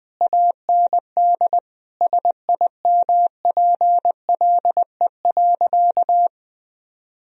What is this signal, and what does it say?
Morse code, heard as a keyed tone, about 20 wpm, 700 Hz. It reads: AND SIMPLE.
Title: Sentences from Top 1000 Words - Review Part 3 - 20wpm